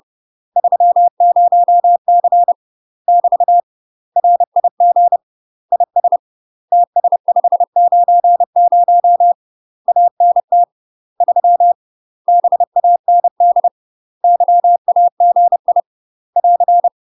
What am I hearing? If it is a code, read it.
Morse code, heard as a keyed tone, about 30 wpm, 700 Hz. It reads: 30C <BT> RIG IS TS590 ANT 3 BAND YAGI <AR>
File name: UK MASTER QSOs TRAINING / 30WPM / UK MASTER QSOs 0030